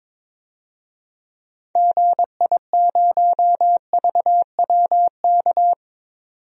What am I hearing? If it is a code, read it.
Morse code, heard as a keyed tone, about 22 wpm, 700 Hz. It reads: GI0VWK
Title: DXCC-100 Call Signs - 6 Characters - 22wpm